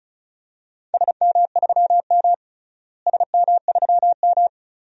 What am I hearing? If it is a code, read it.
Morse code, heard as a keyed tone, about 34 wpm, 700 Hz. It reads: SM3M SM3M
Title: DXCC-100 Call Signs - 4 Characters - Repeated 3x - 35wpm